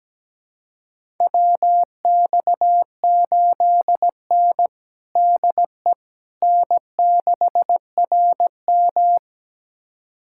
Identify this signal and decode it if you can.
Morse code, heard as a keyed tone, about 17 wpm, 700 Hz. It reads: WX8N DE N6RM
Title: USA Callsign DE Callsign - 4 Characters - 17wpm